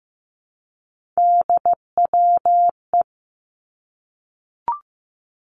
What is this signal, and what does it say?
Morse code, heard as a keyed tone, about 15 wpm, 700 Hz. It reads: DWE
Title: Three Challenging Letter-Number Combinations 15wpm